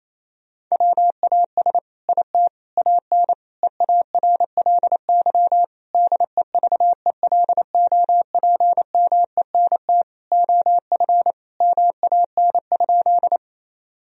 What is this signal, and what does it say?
Morse code, heard as a keyed tone, about 28 wpm, 700 Hz. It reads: WAS IT AN EARLY DEVELOPMENT OF MAN?
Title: Sentences from Top 1000 Words - Review Part 4 - 28wpm